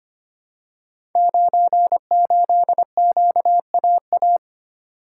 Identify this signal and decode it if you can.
Morse code, heard as a keyed tone, about 25 wpm, 700 Hz. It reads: 98QAA
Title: Code Groups - 5 Letter:Numbers - 25wpm